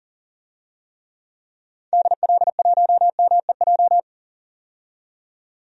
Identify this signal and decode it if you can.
Morse code, heard as a keyed tone, about 40 wpm, 700 Hz. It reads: DL1MEJ
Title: DXCC-100 Call Signs - 6 Characters - 40wpm